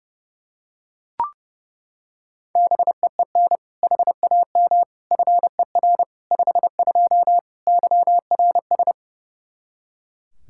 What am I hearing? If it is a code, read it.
Morse code, heard as a keyed tone, about 30 wpm, 700 Hz. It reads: BEEN HAM FER 52 YRS
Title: Practice QSO - Exchange 3 - More Info - 30wpm